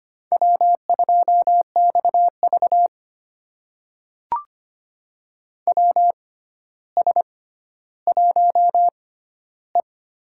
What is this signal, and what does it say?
Morse code, heard as a keyed tone, about 25 wpm, spaced as for 13 wpm, 700 Hz. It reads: W2XV WS1E
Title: USA Call Signs - 4 Characters - 6x Character Spacing - 25wpm